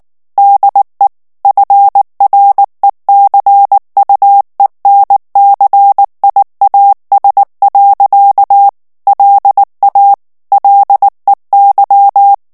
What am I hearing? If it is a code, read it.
Morse code, heard as a keyed tone, about 19 wpm, 800 Hz. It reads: DE FRECUENCIAS. LA LEY
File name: ESPAGNOL TEXTE EN CLAIR 2 VITESSE 18 MOTS MINUTES